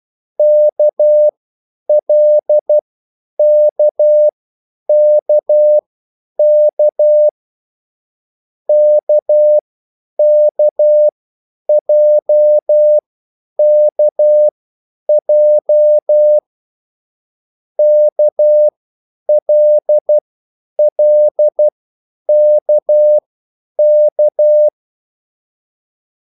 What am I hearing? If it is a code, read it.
Morse code, heard as a keyed tone, about 12 wpm, spaced as for 9 wpm, 600 Hz. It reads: KLKKK KKJKJ KLLKK
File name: QSL / CW lesson